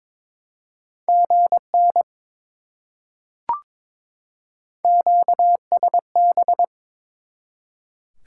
Morse code, heard as a keyed tone, about 22 wpm, 700 Hz. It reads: GN QSB
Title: Top 90 QSO Elements 22wpm